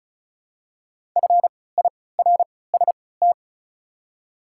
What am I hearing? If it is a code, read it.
Morse code, heard as a keyed tone, about 36 wpm, spaced as for 20 wpm, 700 Hz. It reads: FIRST
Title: 3x Character Spacing - Top 100 Words - 35wpm